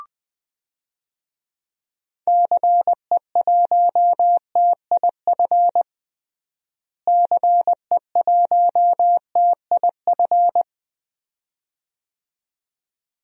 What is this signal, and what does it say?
Morse code, heard as a keyed tone, about 20 wpm, 700 Hz. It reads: CE1TIF CE1TIF
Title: DXCC-100 Call Signs - 6 Characters - Repeated 2x - 20wpm